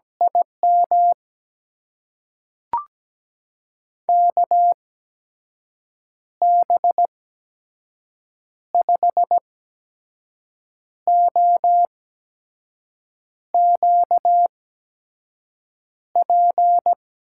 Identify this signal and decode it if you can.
Morse code, heard as a keyed tone, about 17 wpm, spaced as for 5 wpm, 700 Hz. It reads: IM KB5OQP
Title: USA Call Signs - 6 Characters - 8x Character Spacing - 17wpm